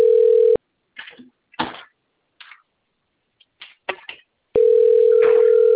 Ringback tone (North America), 440 + 480 Hz.